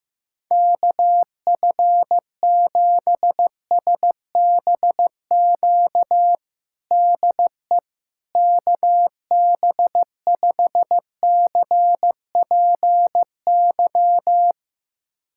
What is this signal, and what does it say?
Morse code, heard as a keyed tone, about 15 wpm, 700 Hz. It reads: KF7SBQ DE KB5CPY
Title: USA Callsign DE Callsign 15wpm